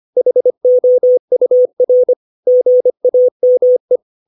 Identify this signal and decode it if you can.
Morse code, heard as a keyed tone, about 25 wpm, 500 Hz. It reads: HOUR GAME
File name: 632 words / 632 Words 25WPM